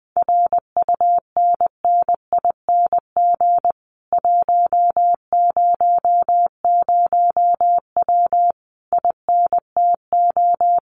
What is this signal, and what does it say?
Morse code, heard as a keyed tone, about 20 wpm, 700 Hz. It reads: RUNNING 100W INTO
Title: Common QSO Phrases Encore 20wpm